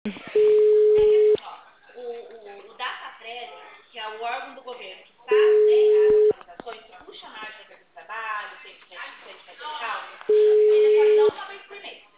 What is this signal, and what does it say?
Ringback tone (Europe, 425 Hz), with speech.
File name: out- -2013- - - .68782